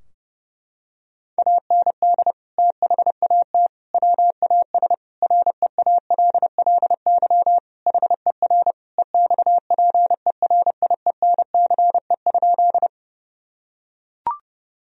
Morse code, heard as a keyed tone, about 30 wpm, 700 Hz. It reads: AND THAT WAS REALLY HER EXPERIENCE?
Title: Sentences from Top 1000 Words - Review Part 3 - 30wpm